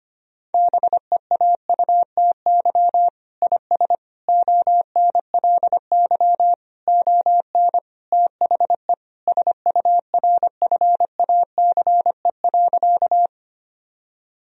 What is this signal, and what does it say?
Morse code, heard as a keyed tone, about 25 wpm, 700 Hz. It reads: BEAUTY IS ONLY ON THE SURFACE.
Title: Sentences from Top 1000 Words - Review Part 3 - 25wpm